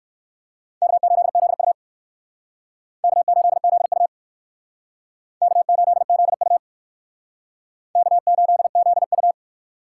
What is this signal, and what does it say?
Morse code, heard as a keyed tone, about 56 wpm, 700 Hz. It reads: K8ZU K8ZU K8ZU K8ZU
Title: Speed Racing - USA 1x2 Call Signs 45wpm